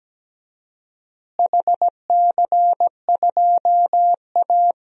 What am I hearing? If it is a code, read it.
Morse code, heard as a keyed tone, about 17 wpm, 700 Hz. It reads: HC2A
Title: DXCC-100 Call Signs - 4 Characters - Repeated 3x - 17wpm